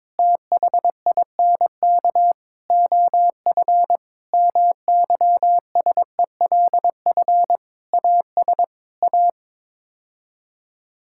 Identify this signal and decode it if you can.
Morse code, heard as a keyed tone, about 22 wpm, 700 Hz. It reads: THINK OF MYSELF AS A
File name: practice / 22wpm / Sets of 8 Words - 22wpm